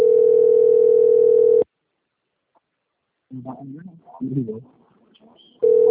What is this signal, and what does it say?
Ringback tone (North America), 440 + 480 Hz, with speech.
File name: rg-1600-unknown- - - .57630